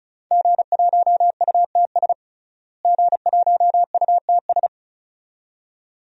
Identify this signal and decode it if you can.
Morse code, heard as a keyed tone, about 36 wpm, 700 Hz. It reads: G1UTS G1UTS
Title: DXCC-100 Call Signs - 5 Characters - Repeated 2x - 35wpm